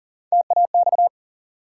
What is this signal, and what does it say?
Morse code, heard as a keyed tone, about 40 wpm, 700 Hz. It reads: TAX
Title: Three Letter Words - 40wpm